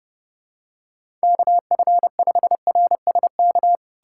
Morse code, heard as a keyed tone, about 30 wpm, 700 Hz. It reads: KF5RSK
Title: USA Callsign DE Callsign - 6 Characters - 30wpm